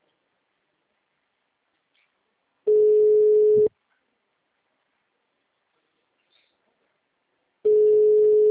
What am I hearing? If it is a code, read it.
Ringback tone (Europe, 425 Hz).
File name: out- -2005- - - .5055